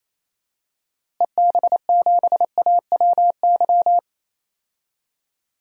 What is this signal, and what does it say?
Morse code, heard as a keyed tone, about 27 wpm, 700 Hz. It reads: EB7AWY
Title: DXCC-100 Call Signs - 6 Characters - Repeated 2x - 28wpm